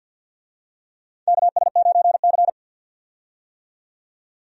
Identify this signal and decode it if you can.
Morse code, heard as a keyed tone, about 51 wpm, 700 Hz. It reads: KS9C
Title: USA Call Signs - 4 Characters - 50wpm